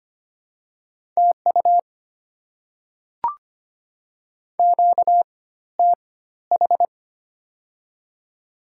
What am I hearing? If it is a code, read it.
Morse code, heard as a keyed tone, about 25 wpm, spaced as for 14 wpm, 700 Hz. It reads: TU QTH
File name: Top 20 QSO Elements - 4x Character Spacing - 25wpm